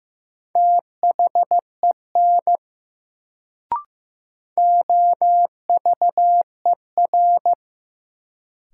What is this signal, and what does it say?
Morse code, heard as a keyed tone, about 15 wpm, 700 Hz. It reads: THEN OVER
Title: Sets of 2 Words - New Word Drill - 15wpm